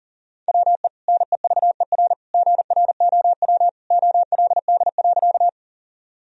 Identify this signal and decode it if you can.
Morse code, heard as a keyed tone, about 40 wpm, 700 Hz. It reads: WE NEVER GROW OLD.
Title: Sentences from Top 1000 Words - Review Part 1 - 40wpm